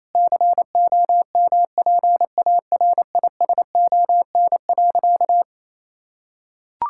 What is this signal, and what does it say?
Morse code, heard as a keyed tone, about 28 wpm, 700 Hz. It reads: COMPARISON.
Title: Sentences from Top 1600 Words 28wpm